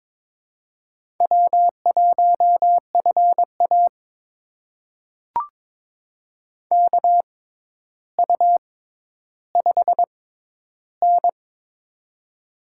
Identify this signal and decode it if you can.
Morse code, heard as a keyed tone, about 22 wpm, spaced as for 11 wpm, 700 Hz. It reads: W1FA KU5N
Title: USA Call Signs - 4 Characters - 6x Character Spacing - 22wpm